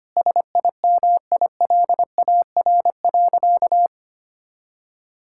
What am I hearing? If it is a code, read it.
Morse code, heard as a keyed tone, about 25 wpm, 700 Hz. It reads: SIMILAR.
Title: Sentences from Top 1000 Words - Review Part 5 - 25wpm